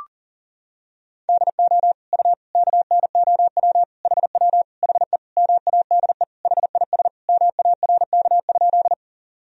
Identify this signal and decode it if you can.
Morse code, heard as a keyed tone, about 40 wpm, 700 Hz. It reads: DO U KNOW HW HE MADE HIS MARK?
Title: Sentences from Top 300 Words and Abbreviations 40wpm